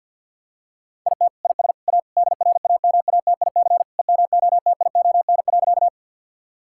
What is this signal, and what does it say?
Morse code, heard as a keyed tone, about 50 wpm, 700 Hz. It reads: IT IS A DRAMATIC EMOTION.
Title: Sentences from Top 1600 Words 50wpm